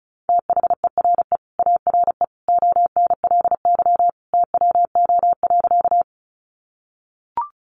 Morse code, heard as a keyed tone, about 34 wpm, 700 Hz. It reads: THERE ARE ONLY TWO.
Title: Sentences from Top 200 Words 35wpm